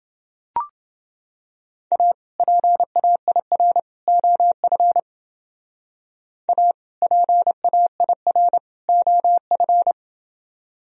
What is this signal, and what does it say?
Morse code, heard as a keyed tone, about 30 wpm, 700 Hz. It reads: A PAIR OF A PAIR OF